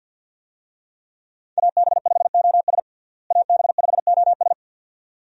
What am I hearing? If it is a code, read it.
Morse code, heard as a keyed tone, about 49 wpm, 700 Hz. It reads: AB5OS AB5OS
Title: USA Call Signs - 5 Characters - Repeated 2x - 50wpm